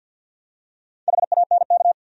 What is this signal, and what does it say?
Morse code, heard as a keyed tone, about 53 wpm, 700 Hz. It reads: HANK
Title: USA Common Names - Max 5 Letters - 50wpm